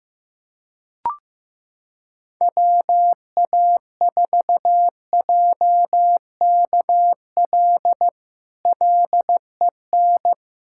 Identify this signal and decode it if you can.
Morse code, heard as a keyed tone, about 15 wpm, 700 Hz. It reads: WA4JKL LEN
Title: K1USN SST QSO Practice - 15wpm